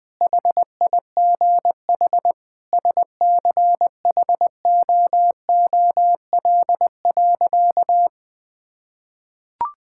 Morse code, heard as a keyed tone, about 20 wpm, 700 Hz. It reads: HIGH SCHOOL.
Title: Sentences with 2-Word Phrases - 20wpm